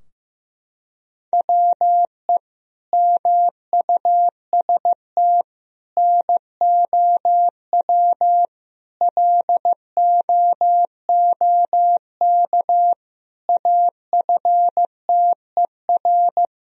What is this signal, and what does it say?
Morse code, heard as a keyed tone, about 15 wpm, 700 Hz. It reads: WE MUST NOW LOOK AFTER
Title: Sentences from Top 1000 Words - Review Part 4 - 15wpm